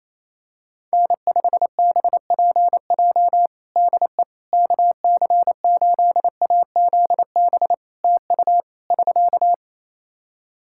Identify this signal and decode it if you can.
Morse code, heard as a keyed tone, about 28 wpm, 700 Hz. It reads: N5BPJ DE KC8AZB TU <SK>
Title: K1USN SST QSO Practice 28wpm